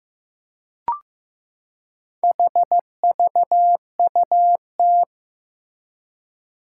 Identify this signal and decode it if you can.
Morse code, heard as a keyed tone, about 15 wpm, 700 Hz. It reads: HVUT
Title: Four Challenging Letter-Number Combinations 15wpm